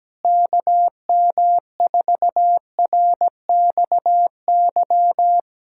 Morse code, heard as a keyed tone, about 17 wpm, 700 Hz. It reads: KM4RXY
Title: USA Callsign DE Callsign 17wpm